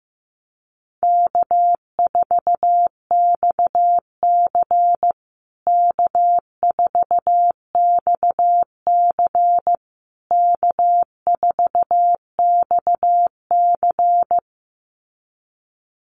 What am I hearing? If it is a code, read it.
Morse code, heard as a keyed tone, about 15 wpm, 700 Hz. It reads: K4XC K4XC K4XC
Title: USA Call Signs Repeated 3x 15wpm